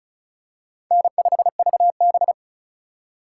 Morse code, heard as a keyed tone, about 36 wpm, 700 Hz. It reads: N5VB
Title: USA Call Signs - 4 Characters - 4x Character Spacing - 35wpm